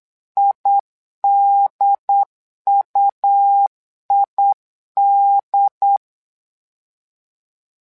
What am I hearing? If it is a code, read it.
Morse code, heard as a keyed tone, about 8 wpm, 800 Hz. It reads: IDUID
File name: LEZIONI / 2 UHDN / 2 UHDN 8wpm2232